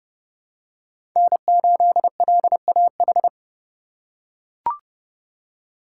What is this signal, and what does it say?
Morse code, heard as a keyed tone, about 30 wpm, 700 Hz. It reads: N8LAH
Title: USA Call Signs - 5 Characters - 3x Character Spacing - 30wpm